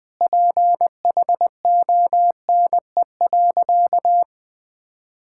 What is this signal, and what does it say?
Morse code, heard as a keyed tone, about 20 wpm, 700 Hz. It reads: PHONE.
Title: Easy Sentences with Hints - 20wpm